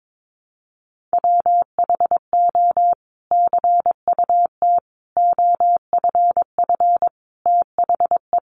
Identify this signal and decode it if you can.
Morse code, heard as a keyed tone, about 22 wpm, 700 Hz. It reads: WHO CUT OFF THE
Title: Sentences from Top 1300 Words 22wpm